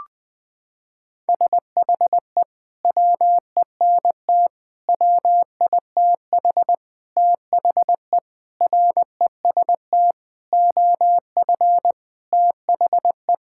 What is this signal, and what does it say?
Morse code, heard as a keyed tone, about 20 wpm, 700 Hz. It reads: SHE WENT WITH THE REST OF THE
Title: Sentences with 5-Word Phrases - 20wpm